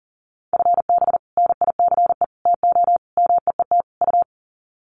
Morse code, heard as a keyed tone, about 40 wpm, 700 Hz. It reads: FB NICE TO MEET U
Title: Common QSO Phrases 40wpm